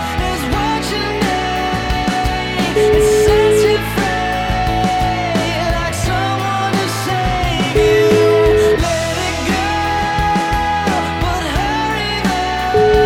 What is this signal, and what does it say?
Ringback tone (Europe, 425 Hz), with noise.